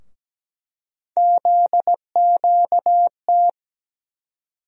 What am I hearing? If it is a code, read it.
Morse code, heard as a keyed tone, about 17 wpm, 700 Hz. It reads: ZQT
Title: USA Call Sign Postfixes - 3 Letters 17wpm